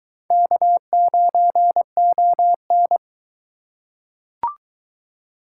Morse code, heard as a keyed tone, about 22 wpm, 700 Hz. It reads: K9ON